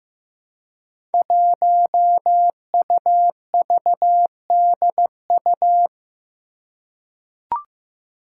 Morse code, heard as a keyed tone, about 15 wpm, 700 Hz. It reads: 1UVDU
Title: Five Challenging Letter-Number Combinations 15wpm